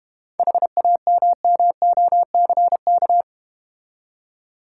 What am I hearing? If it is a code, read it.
Morse code, heard as a keyed tone, about 30 wpm, 700 Hz. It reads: HAMMOCK
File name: Easy Words - Speed-Racing with Hints - 25wpm